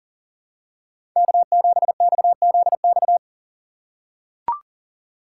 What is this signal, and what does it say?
Morse code, heard as a keyed tone, about 40 wpm, 700 Hz. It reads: K7XZX
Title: USA Call Signs - 5 Characters - 3x Character Spacing - 40wpm